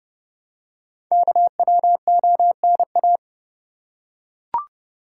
Morse code, heard as a keyed tone, about 30 wpm, 700 Hz. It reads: KWONA
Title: Code Groups - 5 Letters - 30wpm